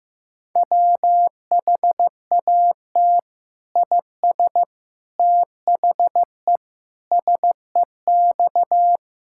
Morse code, heard as a keyed tone, about 15 wpm, 700 Hz. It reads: WHAT IS THE SEX
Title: Sentences from Top 1600 Words 15wpm